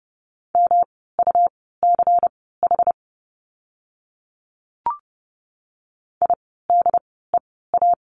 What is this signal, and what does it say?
Morse code, heard as a keyed tone, about 30 wpm, spaced as for 17 wpm, 700 Hz. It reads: MUCH IDEA
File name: course / 30:10wpm / 108 Lesson 18f - Letter m - Words 30wpm